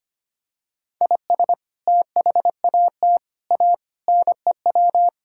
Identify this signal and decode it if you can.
Morse code, heard as a keyed tone, about 25 wpm, 700 Hz. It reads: IS THAT A NEW
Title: Sentences from Top 1000 Words - Review Part 5 - 25wpm